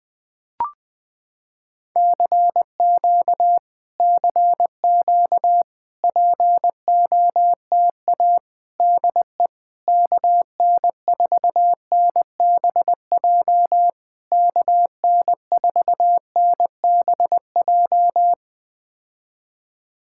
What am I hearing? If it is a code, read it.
Morse code, heard as a keyed tone, about 20 wpm, 700 Hz. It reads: CQ CQ POTA DE KN4NBJ KN4NBJ
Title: POTA QSO Practice - 20wpm